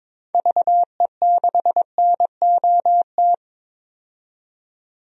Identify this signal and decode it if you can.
Morse code, heard as a keyed tone, about 22 wpm, 700 Hz. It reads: VE6NOT
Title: DXCC-100 Call Signs - 6 Characters - 22wpm